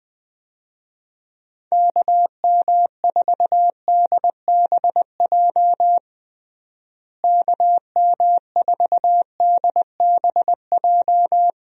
Morse code, heard as a keyed tone, about 20 wpm, 700 Hz. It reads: KM4DBJ KM4DBJ
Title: USA Call Signs - 6 Characters - Repeated 2x - 20wpm